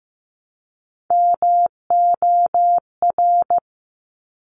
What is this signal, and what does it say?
Morse code, heard as a keyed tone, about 15 wpm, 700 Hz. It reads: MOR
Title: Top Three Letter Combinations in Words 15wpm